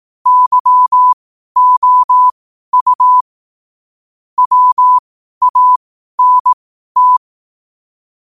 Morse code, heard as a keyed tone, about 18 wpm, spaced as for 12 wpm, 1000 Hz. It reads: YOU WANT
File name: qotdfiles / 12wpm